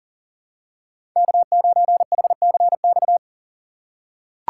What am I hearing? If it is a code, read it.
Morse code, heard as a keyed tone, about 40 wpm, 700 Hz. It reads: K9HCX
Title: USA Call Signs - 3x Character Spacing - 40wpm